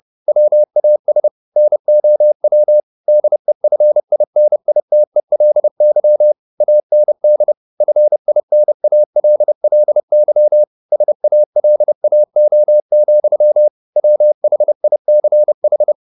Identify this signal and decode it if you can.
Morse code, heard as a keyed tone, about 30 wpm, 600 Hz. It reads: WAS NOW DEFINITELY AND FINALLY SALAO, WHICH
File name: the old man and the sea / the old man and the sea 1 30wpm